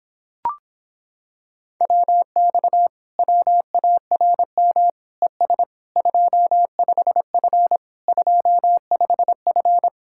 Morse code, heard as a keyed tone, about 26 wpm, 700 Hz. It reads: WX WARM ES 25F 25F